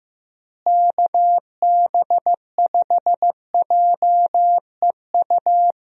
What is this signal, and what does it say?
Morse code, heard as a keyed tone, about 15 wpm, 700 Hz. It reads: KB5JEU
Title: USA Call Signs - 6 Characters - Repeated 2x - 15wpm